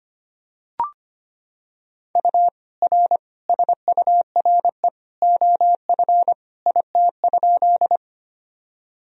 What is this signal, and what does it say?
Morse code, heard as a keyed tone, about 25 wpm, 700 Hz. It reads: U R SURE OF IT?
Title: Sentences from Top 300 Words and Abbreviations 25wpm